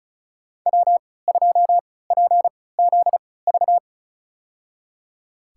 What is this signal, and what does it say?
Morse code, heard as a keyed tone, about 34 wpm, spaced as for 20 wpm, 700 Hz. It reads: W2PZV
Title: USA Call Signs - 5 Characters - 3x Character Spacing - 35wpm